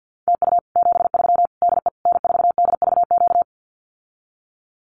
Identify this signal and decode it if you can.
Morse code, heard as a keyed tone, about 48 wpm, 700 Hz. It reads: TU 73 DE N4DVQ
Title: Common QSO Phrases Encore 50wpm